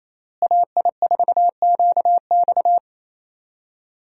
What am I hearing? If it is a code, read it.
Morse code, heard as a keyed tone, about 28 wpm, 700 Hz. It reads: AI4QX
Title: USA Call Signs - 5 Characters - Repeated 2x - 28wpm